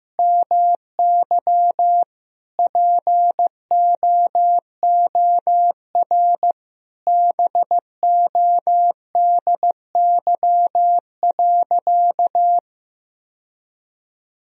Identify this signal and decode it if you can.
Morse code, heard as a keyed tone, about 15 wpm, 700 Hz. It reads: MY POOR BODY.
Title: Sentences from Top 1000 Words - Review Part 2 - 15wpm